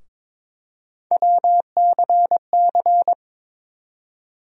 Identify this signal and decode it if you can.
Morse code, heard as a keyed tone, about 21 wpm, 700 Hz. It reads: WCC
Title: USA Call Sign Postfixes - 3 Letters 22wpm